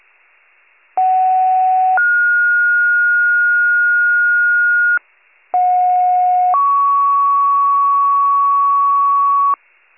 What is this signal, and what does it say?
Two-tone page, heard as a one-second tone, then a three-second tone, about 750 then 1430 Hz, about 730 then 1120 Hz, with noise. It is heard from 1.0 to 9.5 s.